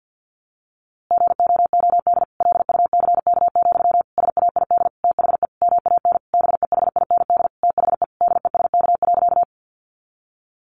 Morse code, heard as a keyed tone, about 50 wpm, 700 Hz. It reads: GOOD LUCK, SAID THE MAN BEHIND THE DESK.
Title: A Sound Of Thunder 50wpm